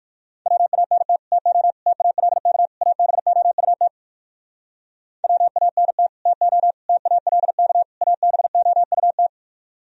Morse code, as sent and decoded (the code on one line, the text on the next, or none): .-- .- -. - / - --- / - .- .-.. -.- / .- -... --- ..- - / .-- .- -. - / - --- / - .- .-.. -.- / .- -... --- ..- -
WANT TO TALK ABOUT WANT TO TALK ABOUT